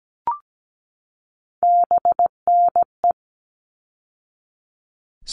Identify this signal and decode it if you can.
Morse code, heard as a keyed tone, about 17 wpm, 700 Hz. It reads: BNE
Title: All Cut Numbers - 3-Digit 17wpm